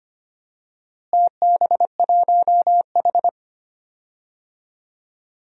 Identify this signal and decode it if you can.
Morse code, heard as a keyed tone, about 25 wpm, 700 Hz. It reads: TB1H
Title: DXCC-100 Call Signs - 4 Characters - Repeated 3x - 25wpm